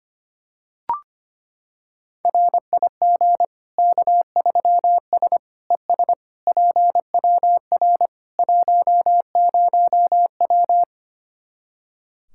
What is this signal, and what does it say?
Morse code, heard as a keyed tone, about 25 wpm, 700 Hz. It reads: RIG K3S ES PWR 10W
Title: Practice QSO - Exchange 2 - Info - 25wpm